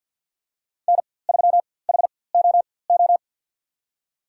Morse code, heard as a keyed tone, about 53 wpm, spaced as for 24 wpm, 700 Hz. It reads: N3HOO
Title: USA Call Signs - 5 Characters - 4x Character Spacing - 50wpm